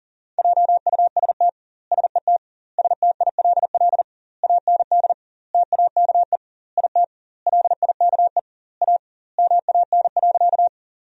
Morse code, heard as a keyed tone, about 40 wpm, 700 Hz. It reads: JUST SET STILL AND TAKE IT LIKE A MAN.
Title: 2x Word-Spacing - Sentences from Top 300 Words - 40wpm